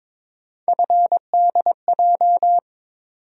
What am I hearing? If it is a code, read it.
Morse code, heard as a keyed tone, about 22 wpm, 700 Hz. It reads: FDJ
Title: USA Call Sign Postfixes - 3 Letters 22wpm